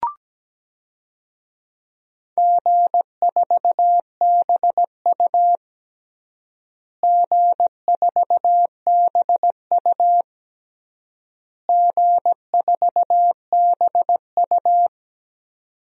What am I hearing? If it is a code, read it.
Morse code, heard as a keyed tone, about 17 wpm, 700 Hz. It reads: G4BU G4BU G4BU
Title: DXCC-100 Call Signs - 4 Characters - Repeated 3x - 17wpm